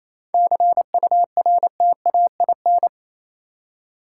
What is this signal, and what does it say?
Morse code, heard as a keyed tone, about 27 wpm, 690 Hz. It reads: CURTAIN
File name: Easy Words - Speed-Racing with Hints - 22wpm